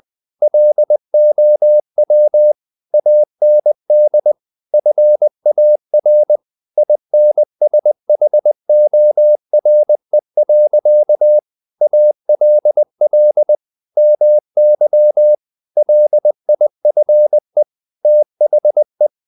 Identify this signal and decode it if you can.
Morse code, heard as a keyed tone, about 20 wpm, 600 Hz. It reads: LOW AND FAR INSHORE. ALL MY LIFE THE
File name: the old man and the sea / the old man and the sea part 1 20wpm